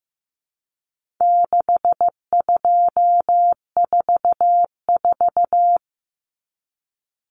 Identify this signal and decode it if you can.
Morse code, heard as a keyed tone, about 15 wpm, 700 Hz. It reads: 6244
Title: Numbers - 4-Digit 15wpm